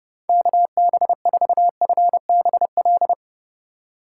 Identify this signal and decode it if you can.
Morse code, heard as a keyed tone, about 30 wpm, 700 Hz. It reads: KB4FBL
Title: USA Call Signs - 6 Characters - 2x Character Spacing - 30wpm